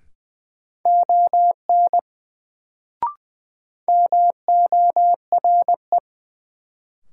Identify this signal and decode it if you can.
Morse code, heard as a keyed tone, about 20 wpm, 700 Hz. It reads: ON MORE
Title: Sentences from Top 100 Words - New Word Drill - 20wpm